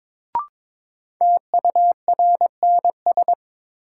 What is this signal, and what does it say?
Morse code, heard as a keyed tone, about 22 wpm, 700 Hz. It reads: TURNS
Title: Sets of 5 Words Encore - New Word Drill - 22wpm